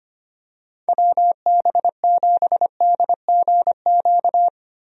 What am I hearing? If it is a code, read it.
Morse code, heard as a keyed tone, about 25 wpm, 700 Hz. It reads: WB7DGQ
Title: USA Call Signs - 6 Characters - Repeated 2x - 25wpm